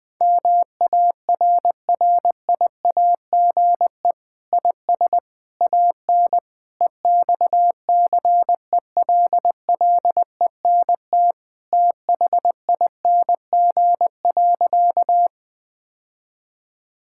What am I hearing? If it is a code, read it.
Morse code, heard as a keyed tone, about 20 wpm, 700 Hz. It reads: MARRIAGE IS AN EXCELLENT THING.
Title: Sentences from Top 1000 Words - Review Part 5 - 20wpm